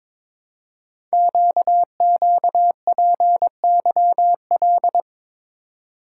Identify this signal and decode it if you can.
Morse code, heard as a keyed tone, about 23 wpm, 700 Hz. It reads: QQPYL
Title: Code Groups - 5 Letters - 22wpm